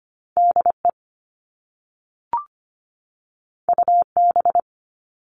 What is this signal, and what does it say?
Morse code, heard as a keyed tone, about 25 wpm, 700 Hz. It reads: DE UB
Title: All Cut Numbers - 2-Digit 25wpm